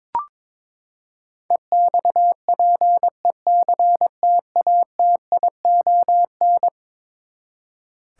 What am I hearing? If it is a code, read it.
Morse code, heard as a keyed tone, about 21 wpm, 700 Hz. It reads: EXPECTATION
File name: practice / 22wpm / Eleven Letter Words - 22wpm